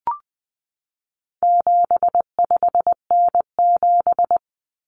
Morse code, heard as a keyed tone, about 20 wpm, 700 Hz. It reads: 75N7
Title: Common Cut Numbers - 4-Digit 20wpm